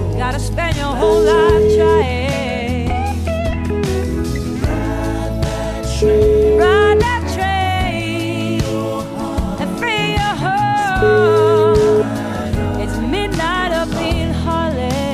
Ringback tone (Europe, 425 Hz), with noise.